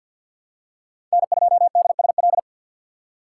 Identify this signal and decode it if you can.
Morse code, heard as a keyed tone, about 53 wpm, 700 Hz. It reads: N2DSL
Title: USA Call Signs - 5 Characters - 2x Character Spacing - 50wpm